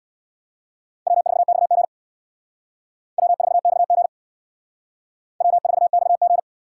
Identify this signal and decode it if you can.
Morse code, heard as a keyed tone, about 62 wpm, 700 Hz. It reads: W4XG W4XG W4XG
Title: Speed Racing - USA 1x2 Call Signs 50wpm